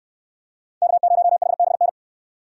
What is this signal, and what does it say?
Morse code, heard as a keyed tone, about 66 wpm, 700 Hz. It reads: K0HBN